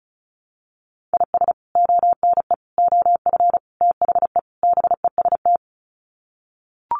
Morse code, heard as a keyed tone, about 34 wpm, 700 Hz. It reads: IS ONE OF THE BEST
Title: Sets of 5 Words 35wpm